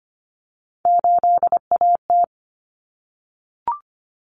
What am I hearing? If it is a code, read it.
Morse code, heard as a keyed tone, about 25 wpm, 700 Hz. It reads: 8AT
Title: Common Cut Numbers - 3-Digit 25wpm